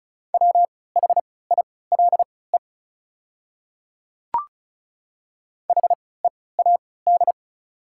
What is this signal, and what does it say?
Morse code, heard as a keyed tone, about 34 wpm, spaced as for 20 wpm, 700 Hz. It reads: WHILE HEAD
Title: 3x Character Spacing - Top 300 Words - 35wpm